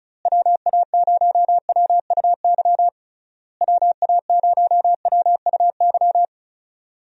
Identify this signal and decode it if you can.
Morse code, heard as a keyed tone, about 34 wpm, 700 Hz. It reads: WA0WUY WA0WUY
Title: USA Call Signs - 6 Characters - Repeated 3x - 35wpm